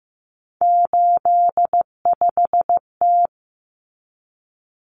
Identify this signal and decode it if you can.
Morse code, heard as a keyed tone, about 15 wpm, 700 Hz. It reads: 85T
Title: Common Cut Numbers - 3-Digit 15wpm